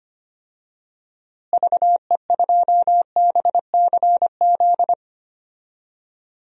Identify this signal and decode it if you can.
Morse code, heard as a keyed tone, about 25 wpm, 700 Hz. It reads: VE2BCZ
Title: DXCC-100 Call Signs - 6 Characters - Repeated 2x - 25wpm